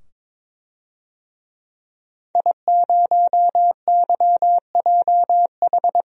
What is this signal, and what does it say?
Morse code, heard as a keyed tone, about 22 wpm, 700 Hz. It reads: I0YJH
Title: DXCC-100 Call Signs - 5 Characters - 22wpm